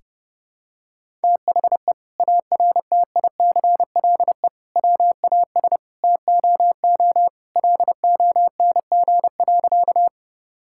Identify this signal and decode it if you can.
Morse code, heard as a keyed tone, about 32 wpm, 700 Hz. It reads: THE ARTICLE WAS TOO LONG.
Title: Sentences from Top 1000 Words - Review Part 5 - 30wpm